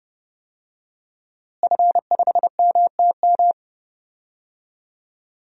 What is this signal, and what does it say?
Morse code, heard as a keyed tone, about 32 wpm, 700 Hz. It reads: F5MTM
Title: DXCC-100 Call Signs - 5 Characters - Repeated 3x - 30wpm